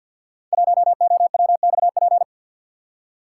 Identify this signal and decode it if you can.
Morse code, heard as a keyed tone, about 50 wpm, 700 Hz. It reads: 1OWXP
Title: Code Groups - 5 Letter:Numbers - 50wpm